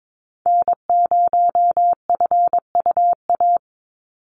Morse code, heard as a keyed tone, about 22 wpm, 700 Hz. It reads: N0FUA
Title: USA Call Signs 22wpm